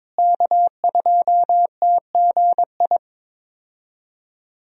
Morse code, heard as a keyed tone, about 22 wpm, 700 Hz. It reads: K2TGI
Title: USA Call Signs - 5 Characters - Repeated 3x - 22wpm